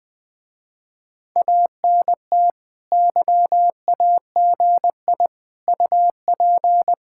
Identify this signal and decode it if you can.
Morse code, heard as a keyed tone, about 22 wpm, 700 Hz. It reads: ANT YAGI UP
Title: Speed-Racing Common QSO Phrases Encore - 20wpm